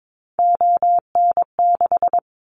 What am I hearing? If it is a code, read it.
Morse code, heard as a keyed tone, about 22 wpm, 700 Hz. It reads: ON6
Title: Callsign Prefixes for Top 50 DXCC - 22wpm